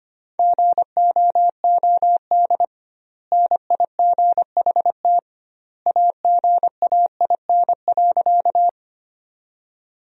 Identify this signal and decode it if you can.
Morse code, heard as a keyed tone, about 25 wpm, 700 Hz. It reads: GOOD NIGHT AGAIN.
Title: 2x Word-Spacing - Sentences from Top 300 Words - 25wpm